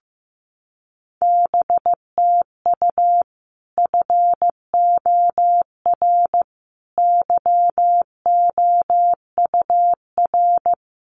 Morse code, heard as a keyed tone, about 15 wpm, 700 Hz. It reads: BTU FOR YOUR
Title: Common QSO Phrases 15wpm